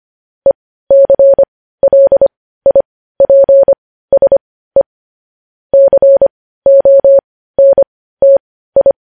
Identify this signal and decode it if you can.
Morse code, heard as a keyed tone, about 25 wpm, spaced as for 15 wpm, 550 Hz. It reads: ECLIPSE CONTI
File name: podcast / News headlines-1x-25x15